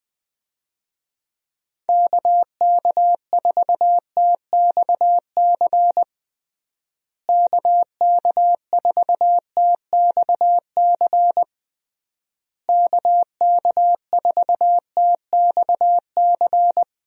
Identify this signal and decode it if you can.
Morse code, heard as a keyed tone, about 20 wpm, 700 Hz. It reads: KK4TXC KK4TXC KK4TXC
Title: USA Call Signs - 6 Characters - Repeated 3x - 20wpm